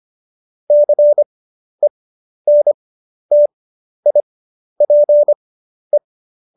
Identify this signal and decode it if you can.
Morse code, heard as a keyed tone, about 25 wpm, spaced as for 11 wpm, 590 Hz. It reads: CENTIPE
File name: Jokes-25-char-10-eff